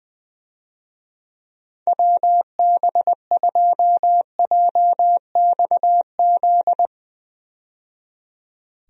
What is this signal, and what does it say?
Morse code, heard as a keyed tone, about 20 wpm, 700 Hz. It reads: WB2JXZ
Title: USA Call Signs - 6 Characters - 20wpm